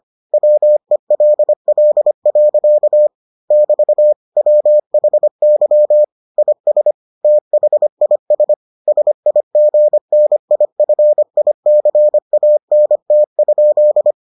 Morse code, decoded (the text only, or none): WELL. <BT> WHY IS THIS SIGNIFICANT?